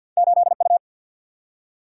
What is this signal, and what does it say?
Morse code, heard as a keyed tone, about 47 wpm, 700 Hz. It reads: 8 U
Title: Single-Letter Single-Number 50wpm